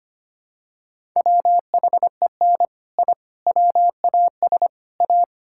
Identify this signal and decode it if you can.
Morse code, heard as a keyed tone, about 25 wpm, 700 Hz. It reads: WHEN I WAS A
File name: practice / 25wpm / Sentences with 3-Word Phrases - 25wpm